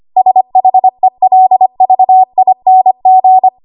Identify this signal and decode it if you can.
Morse code, heard as a keyed tone, about 25 wpm, 750 Hz. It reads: SHELVING
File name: Morse / 25WPM